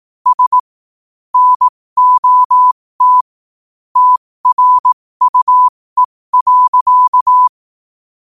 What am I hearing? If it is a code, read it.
Morse code, heard as a keyed tone, about 18 wpm, spaced as for 15 wpm, 1000 Hz. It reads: S NOT TRUE.
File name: qotdfiles / 15wpm